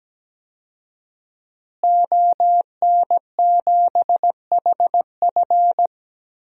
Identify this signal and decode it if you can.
Morse code, heard as a keyed tone, about 17 wpm, 700 Hz. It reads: ON7HF
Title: DXCC-100 Call Signs - 5 Characters - Repeated 3x - 17wpm